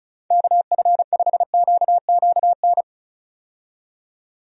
Morse code, heard as a keyed tone, about 36 wpm, 700 Hz. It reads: KF5QQN
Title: USA Call Signs - 6 Characters - 35wpm